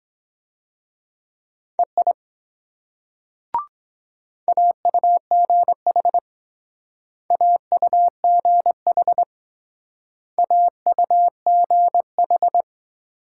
Morse code, heard as a keyed tone, about 21 wpm, 700 Hz. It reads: EI AUGH AUGH AUGH
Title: Speed-Racing Multi-letter Phonemes - 17wpm